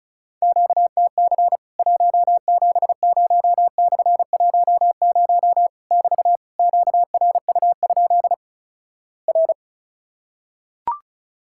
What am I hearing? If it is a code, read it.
Morse code, heard as a keyed tone, about 36 wpm, 700 Hz. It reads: QTC 170/10 <BT> QRU? R
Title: Worked All Europe DX Contest - QTC Exchange - 35wpm